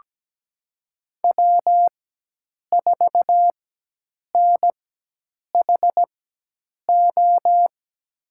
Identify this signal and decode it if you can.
Morse code, heard as a keyed tone, about 17 wpm, spaced as for 8 wpm, 700 Hz. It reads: W4NHO
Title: USA Call Signs - 5 Characters - 4x Character Spacing - 17wpm